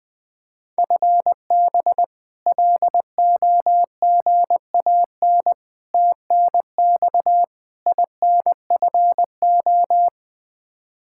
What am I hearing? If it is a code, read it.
Morse code, heard as a keyed tone, about 20 wpm, 700 Hz. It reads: FB LOGAN TNX INFO
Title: K1USN SST QSO Practice 20wpm